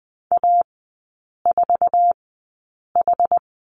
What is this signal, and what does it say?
Morse code, heard as a keyed tone, about 20 wpm, spaced as for 8 wpm, 690 Hz. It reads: A4H
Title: 145 Lesson 24e - Letter y - Three Characters 20wpm